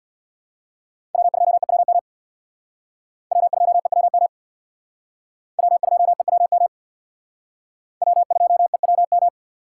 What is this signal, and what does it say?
Morse code, heard as a keyed tone, about 60 wpm, 700 Hz. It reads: W2EWM W2EWM W2EWM W2EWM
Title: Speed Racing - USA 1x3 Call Signs 50wpm